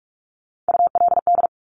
Morse code, heard as a keyed tone, about 46 wpm, 700 Hz. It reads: ULD
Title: Top Three Letter Combinations in Words 45wpm